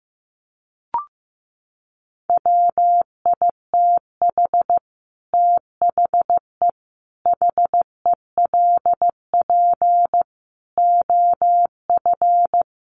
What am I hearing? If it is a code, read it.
Morse code, heard as a keyed tone, about 15 wpm, 700 Hz. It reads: WITH THE HELP OF
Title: Sets of 5 Words 15wpm